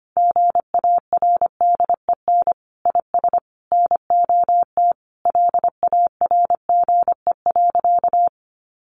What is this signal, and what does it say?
Morse code, heard as a keyed tone, about 25 wpm, 700 Hz. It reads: GARDEN IS NOT LARGE.
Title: Sentences from Top 700 Words 25wpm